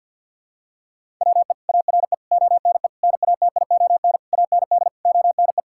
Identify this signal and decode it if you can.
Morse code, heard as a keyed tone, about 52 wpm, 700 Hz. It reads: WE ARE ONE NATION AND ONE
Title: Sentences from Top 1000 Words - Review Part 6 - 50wpm